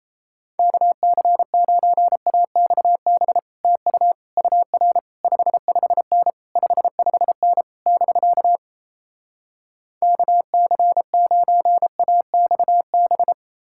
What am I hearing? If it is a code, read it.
Morse code, heard as a keyed tone, about 37 wpm, 700 Hz. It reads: KC9AXB TU UR 55N 55N <BK> KC9AXB